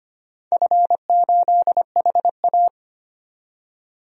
Morse code, heard as a keyed tone, about 25 wpm, 700 Hz. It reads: F8HA
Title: DXCC-100 Call Signs - 4 Characters - 25wpm